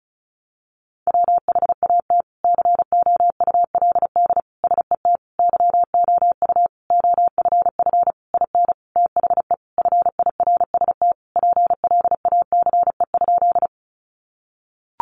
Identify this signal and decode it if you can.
Morse code, heard as a keyed tone, about 34 wpm, 700 Hz. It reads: WHAT COULD SET YOU OFF IN THE FIRST PLACE?
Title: Sentences from Top 200 Words 35wpm